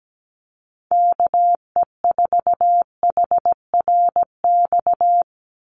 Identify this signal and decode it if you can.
Morse code, heard as a keyed tone, about 17 wpm, 700 Hz. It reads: KE4HRX
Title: USA Call Signs 17wpm